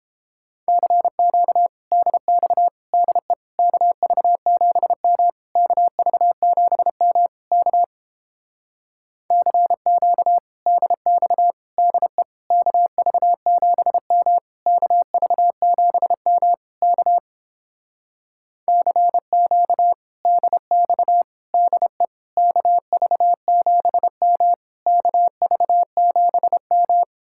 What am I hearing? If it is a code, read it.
Morse code, heard as a keyed tone, about 29 wpm, 700 Hz. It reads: CQ DX DE KV7M KV7M K CQ DX DE KV7M KV7M K CQ DX DE KV7M KV7M